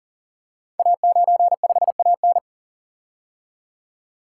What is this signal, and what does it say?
Morse code, heard as a keyed tone, about 42 wpm, 700 Hz. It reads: A95AN
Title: Code Groups - 5 Letter:Numbers - 40wpm